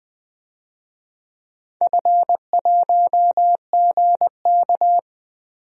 Morse code, heard as a keyed tone, about 20 wpm, 700 Hz. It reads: F1GK
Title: DXCC-100 Call Signs - 4 Characters - 20wpm